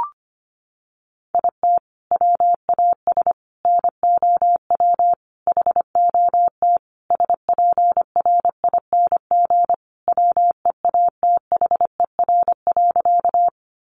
Morse code, heard as a keyed tone, about 25 wpm, 700 Hz. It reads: IT WAS NOW HOT SPRING WEATHER.
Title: Sentences from Top 1200 Words 25wpm